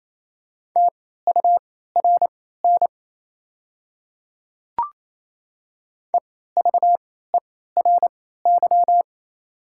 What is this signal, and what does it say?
Morse code, heard as a keyed tone, about 27 wpm, spaced as for 16 wpm, 690 Hz. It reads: TURN EVERY
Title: 3x Character Spacing - Top 200 Words - 28wpm